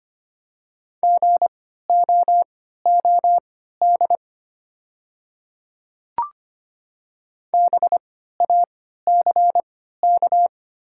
Morse code, heard as a keyed tone, about 25 wpm, spaced as for 14 wpm, 690 Hz. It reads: GOOD BACK
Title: 3x Character Spacing - Top 200 Words - 25wpm